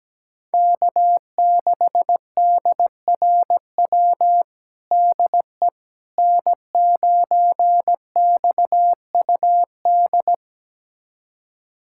Morse code, heard as a keyed tone, about 17 wpm, 700 Hz. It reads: K6DRW DE N9XUD
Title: USA Callsign DE Callsign - 5 Characters - 17wpm